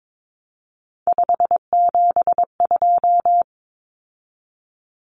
Morse code, heard as a keyed tone, about 22 wpm, 700 Hz. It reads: 572
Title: Common Cut Numbers - 3-Digit 22wpm